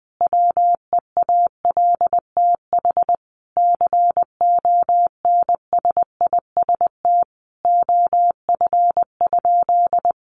Morse code, heard as a keyed tone, about 20 wpm, 700 Hz. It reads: WEALTH CONSIST OF?
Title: Sentences from Top 1400 Words 20wpm